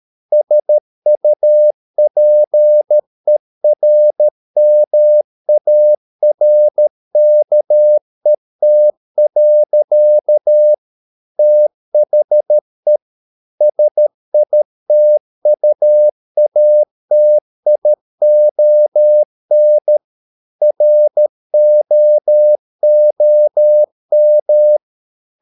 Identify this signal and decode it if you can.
Morse code, heard as a keyed tone, about 13 wpm, 600 Hz. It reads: SUPERMARKET. THE SITUATION ROOM